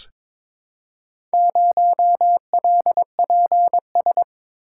Morse code, heard as a keyed tone, about 22 wpm, 700 Hz. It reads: 0LPS
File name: practice / 22wpm / Four Letter-Number Combinations - 22wpm